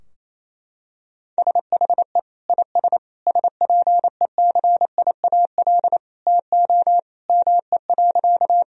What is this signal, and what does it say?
Morse code, heard as a keyed tone, about 28 wpm, 700 Hz. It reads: SHE IS SPECIAL TO ME.
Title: Sentences from Top 1000 Words - Review Part 3 - 28wpm